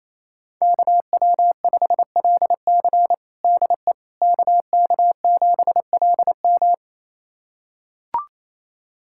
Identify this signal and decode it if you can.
Morse code, heard as a keyed tone, about 28 wpm, 700 Hz. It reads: KW5LC DE KK7LM
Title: USA Callsign DE Callsign - 5 Characters - 28wpm